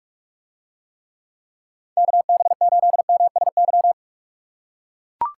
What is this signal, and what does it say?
Morse code, heard as a keyed tone, about 44 wpm, 700 Hz. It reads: KB8MSY